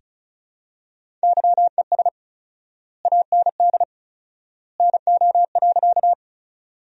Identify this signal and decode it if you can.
Morse code, heard as a keyed tone, about 34 wpm, 700 Hz. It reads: YES AND NO.
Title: 4x Word-Spacing - Sentences from Top 500 Words - 35wpm